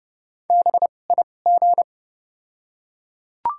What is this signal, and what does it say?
Morse code, heard as a keyed tone, about 31 wpm, spaced as for 22 wpm, 700 Hz. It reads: BIG
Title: 2x Character Spacing - Top 200 Words - 30wpm